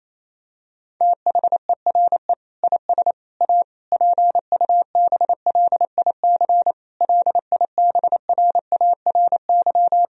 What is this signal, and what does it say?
Morse code, heard as a keyed tone, about 28 wpm, 700 Hz. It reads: THERE IS A PUBLIC LIBRARY
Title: Sentences from Top 1000 Words - Review Part 4 - 28wpm